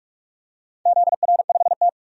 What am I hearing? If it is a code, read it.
Morse code, heard as a keyed tone, about 41 wpm, 700 Hz. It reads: ZR5T
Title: DXCC-100 Call Signs - 4 Characters - 45wpm